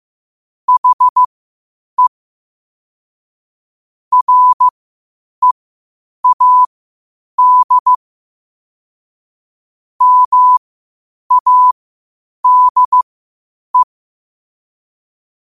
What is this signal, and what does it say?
Morse code, heard as a keyed tone, about 15 wpm, spaced as for 8 wpm, 1000 Hz. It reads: HE READ MADE
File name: qotdfiles / 8wpm